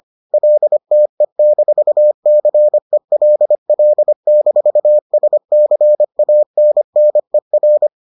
Morse code, heard as a keyed tone, about 25 wpm, 600 Hz. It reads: LTE-CELL-SCANNER